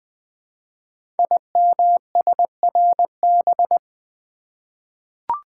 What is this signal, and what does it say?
Morse code, heard as a keyed tone, about 20 wpm, 700 Hz. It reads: IMSRB
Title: Code Groups - 5 Letters - 20wpm